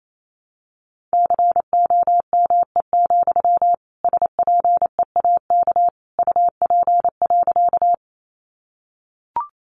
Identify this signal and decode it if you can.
Morse code, heard as a keyed tone, about 28 wpm, 700 Hz. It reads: COME, SPEAK UP.
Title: Sentences from Top 1100 Words 28wpm